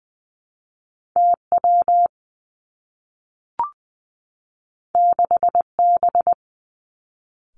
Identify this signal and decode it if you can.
Morse code, heard as a keyed tone, about 20 wpm, 700 Hz. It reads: T W 6 B
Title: Two Challenging Letter-Number Combinations 20wpm